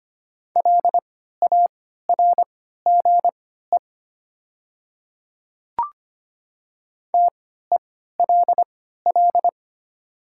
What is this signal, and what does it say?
Morse code, heard as a keyed tone, about 25 wpm, spaced as for 14 wpm, 700 Hz. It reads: LARGE TELL
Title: 3x Character Spacing - Top 300 Words - 25wpm